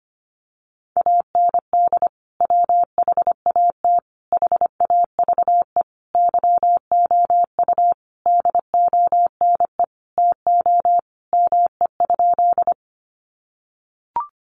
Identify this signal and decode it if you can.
Morse code, heard as a keyed tone, about 25 wpm, 700 Hz. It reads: AND WHAT HAVE YOU DONE TO ME?
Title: Sentences from Top 500 Words 25wpm